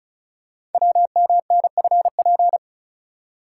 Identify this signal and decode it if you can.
Morse code, heard as a keyed tone, about 36 wpm, 690 Hz. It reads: WMNFP
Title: Code Groups - 5 Letters - 35wpm